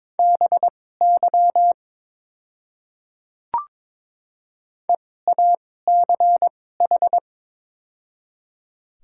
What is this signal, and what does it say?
Morse code, heard as a keyed tone, about 22 wpm, spaced as for 16 wpm, 700 Hz. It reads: BY EACH
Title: 2x Character Spacing - Top 100 Words - 22wpm